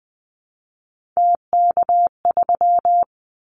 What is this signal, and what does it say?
Morse code, heard as a keyed tone, about 20 wpm, 700 Hz. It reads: TK3
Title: Callsign Prefixes for Top 100 DXCC - 20wpm